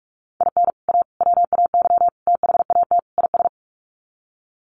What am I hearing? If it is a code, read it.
Morse code, heard as a keyed tone, about 44 wpm, 700 Hz. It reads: IN A WAY THAT IS
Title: Sets of 5 Words Encore 45wpm